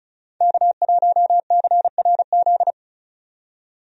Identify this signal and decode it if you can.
Morse code, heard as a keyed tone, about 34 wpm, 700 Hz. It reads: K1CRZ
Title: USA Call Signs - 5 Characters - 35wpm